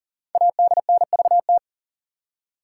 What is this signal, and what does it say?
Morse code, heard as a keyed tone, about 42 wpm, 700 Hz. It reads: ADNVT
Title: Code Groups - 5 Letters - 40wpm